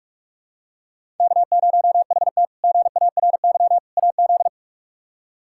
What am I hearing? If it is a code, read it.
Morse code, heard as a keyed tone, about 41 wpm, 700 Hz. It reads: K0HT GARY AZ
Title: K1USN SST QSO Practice - 45wpm